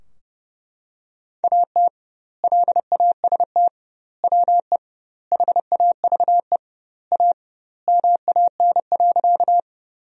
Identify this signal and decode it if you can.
Morse code, heard as a keyed tone, about 30 wpm, 700 Hz. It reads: AT LAST WE HAVE A MAN.
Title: 2x Word-Spacing - Sentences from Top 300 Words - 30wpm